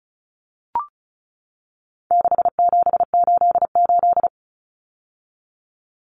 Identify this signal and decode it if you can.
Morse code, heard as a keyed tone, about 34 wpm, 700 Hz. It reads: 6788
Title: Numbers - 4-Digit 35wpm